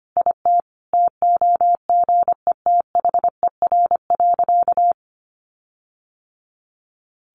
Morse code, heard as a keyed tone, about 25 wpm, 700 Hz. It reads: IT TOGETHER.
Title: Sentences from Top 1500 Words 25wpm